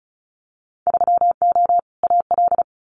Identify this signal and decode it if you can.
Morse code, heard as a keyed tone, about 36 wpm, 700 Hz. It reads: 3O AL
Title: Winter Field Day Exchanges Encore 35wpm